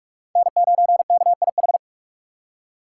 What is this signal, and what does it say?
Morse code, heard as a keyed tone, about 46 wpm, 690 Hz. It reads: N9KIH
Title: USA Call Signs - 5 Characters - 2x Character Spacing - 45wpm